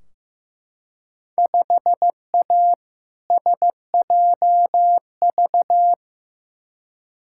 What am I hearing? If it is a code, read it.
Morse code, heard as a keyed tone, about 15 wpm, 700 Hz. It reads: 5A SJV
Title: Field Day Exchanges - 15wpm